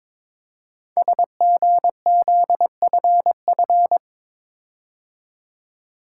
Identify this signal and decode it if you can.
Morse code, heard as a keyed tone, about 22 wpm, 700 Hz. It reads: SGZFF
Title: Code Groups - 5 Letters - 22wpm